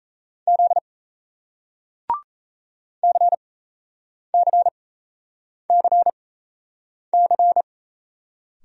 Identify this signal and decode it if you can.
Morse code, heard as a keyed tone, about 38 wpm, spaced as for 8 wpm, 700 Hz. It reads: Z CCCC